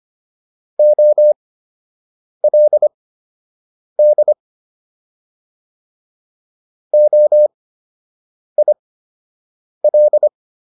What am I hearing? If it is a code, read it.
Morse code, heard as a keyed tone, about 25 wpm, spaced as for 7 wpm, 600 Hz. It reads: OLD OIL